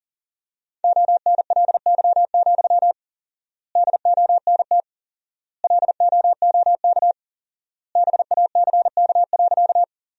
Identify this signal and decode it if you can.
Morse code, heard as a keyed tone, about 37 wpm, 700 Hz. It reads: ONLY, DONT LOOK BACK.
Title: 4x Word-Spacing - Sentences from Top 300 Words - 40wpm